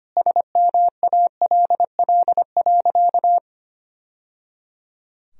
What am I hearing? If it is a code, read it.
Morse code, heard as a keyed tone, about 25 wpm, 700 Hz. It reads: SMALL.
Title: Sentences from Top 1000 Words - Review Part 4 - 25wpm